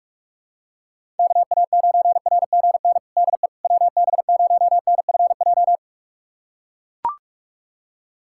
Morse code, heard as a keyed tone, about 41 wpm, 700 Hz. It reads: KA9RGN DE WB0NFJ
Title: USA Callsign DE Callsign - 6 Characters - 45wpm